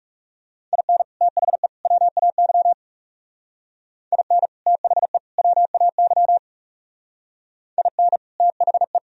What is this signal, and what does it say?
Morse code, heard as a keyed tone, about 38 wpm, 700 Hz. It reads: IN THE WAY IN THE WAY IN THE
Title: Speed-Racing - Sets of 3 Words 30wpm